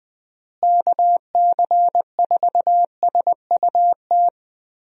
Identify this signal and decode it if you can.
Morse code, heard as a keyed tone, about 20 wpm, 700 Hz. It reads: KC4SUT
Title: USA Call Signs - 6 Characters - Repeated 2x - 20wpm